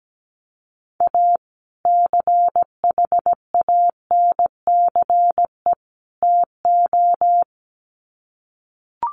Morse code, heard as a keyed tone, about 17 wpm, 700 Hz. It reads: A CHANCE TO
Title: Sets of 3 Words 17wpm